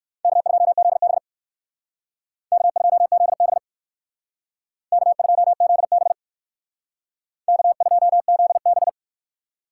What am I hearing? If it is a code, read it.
Morse code, heard as a keyed tone, about 57 wpm, 700 Hz. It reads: K2ZB K2ZB K2ZB K2ZB